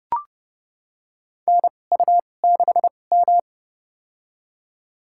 Morse code, heard as a keyed tone, about 29 wpm, spaced as for 21 wpm, 700 Hz. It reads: NU6M
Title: USA Call Signs - 2x Character Spacing - 30wpm